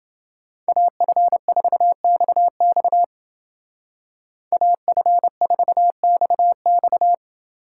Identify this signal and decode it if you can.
Morse code, heard as a keyed tone, about 29 wpm, 700 Hz. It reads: AF4XX AF4XX